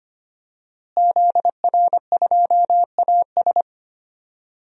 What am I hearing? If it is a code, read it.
Morse code, heard as a keyed tone, about 25 wpm, 700 Hz. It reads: ZR2AS
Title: DXCC-100 Call Signs - 5 Characters - Repeated 3x - 25wpm